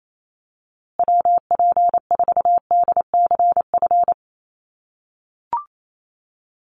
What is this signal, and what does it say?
Morse code, heard as a keyed tone, about 28 wpm, 700 Hz. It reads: WP4DCF
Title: USA Call Signs Repeated 2x 28wpm